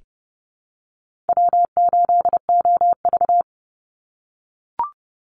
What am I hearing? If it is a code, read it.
Morse code, heard as a keyed tone, about 30 wpm, 700 Hz. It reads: W8OV
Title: Easy Call Signs - 30wpm